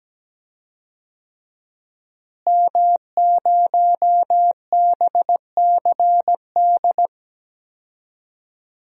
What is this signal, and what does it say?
Morse code, heard as a keyed tone, about 17 wpm, 700 Hz. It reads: M0BCD
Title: DXCC-100 Call Signs - 5 Characters - Repeated 3x - 17wpm